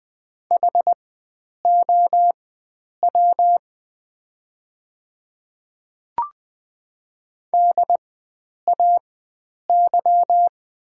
Morse code, heard as a keyed tone, about 20 wpm, spaced as for 9 wpm, 700 Hz. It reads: HOW DAY
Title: 4x Character Spacing - Top 100 Words - 20wpm